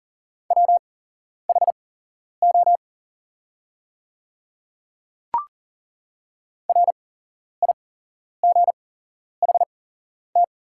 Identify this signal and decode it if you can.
Morse code, heard as a keyed tone, about 41 wpm, spaced as for 11 wpm, 700 Hz. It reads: WHO RIGHT
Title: 8x Character Spacing - Top 200 Words - 40wpm